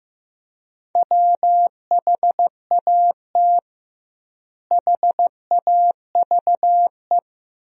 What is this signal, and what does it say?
Morse code, heard as a keyed tone, about 15 wpm, 700 Hz. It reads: WHAT HAVE
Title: 2x Word-Spacing - Sentences from Top 100 Words - 15wpm